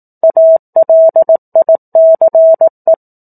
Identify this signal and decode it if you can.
Morse code, heard as a keyed tone, about 18 wpm, 650 Hz. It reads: ALICE
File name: 04-Chapter-04 18 WPM